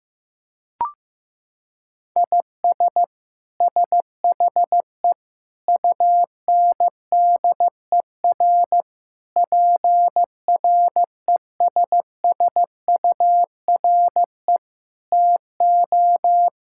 Morse code, heard as a keyed tone, about 15 wpm, 700 Hz. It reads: IS SHE UNDER PRESSURE TO
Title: Sentences from Top 1000 Words - Review Part 6 - 15wpm